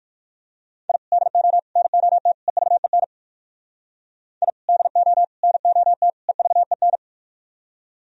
Morse code, heard as a keyed tone, about 51 wpm, 700 Hz. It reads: I DO NOT EVEN I DO NOT EVEN
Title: Speed-Racing - Sets of 4 Words 45wpm